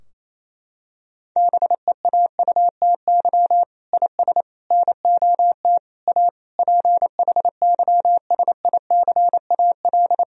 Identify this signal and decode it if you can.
Morse code, heard as a keyed tone, about 28 wpm, 700 Hz. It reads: BEAUTY IS NOT A PHYSICAL
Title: Sentences from Top 1000 Words - Review Part 4 - 28wpm